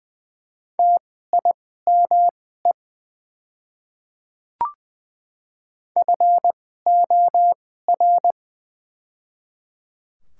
Morse code, heard as a keyed tone, about 20 wpm, spaced as for 14 wpm, 700 Hz. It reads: TIME FOR
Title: 2x Character Spacing - Top 100 Words - 20wpm